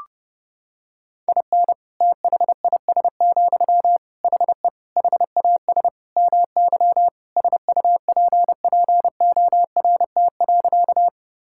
Morse code, heard as a keyed tone, about 30 wpm, 700 Hz. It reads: IN THIS, HE HAS MY SUPPORT.
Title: Sentences from Top 1000 Words - Review Part 5 - 30wpm